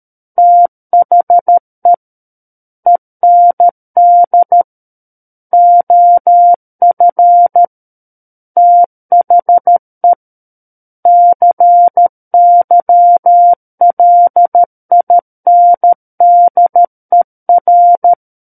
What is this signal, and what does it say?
Morse code, heard as a keyed tone, about 13 wpm, 700 Hz. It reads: THE END OF THE CYLINDER
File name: Book One / 04-Chapter 4 - 13 WPM